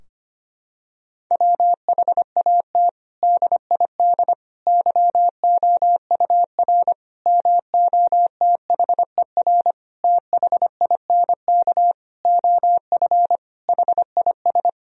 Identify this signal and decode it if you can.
Morse code, heard as a keyed tone, about 25 wpm, 700 Hz. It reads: WHAT DID YOUR MOTHER THINK OF HIS
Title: Sentences from Top 1000 Words - Review Part 4 - 25wpm